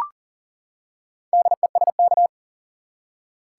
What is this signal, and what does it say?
Morse code, heard as a keyed tone, about 40 wpm, 700 Hz. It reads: DESK
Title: Four Letter Words - 40wpm